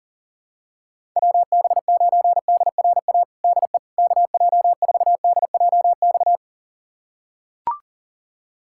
Morse code, heard as a keyed tone, about 40 wpm, 700 Hz. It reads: WB9DRA DE KJ4DJX
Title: USA Callsign DE Callsign - 6 Characters - 40wpm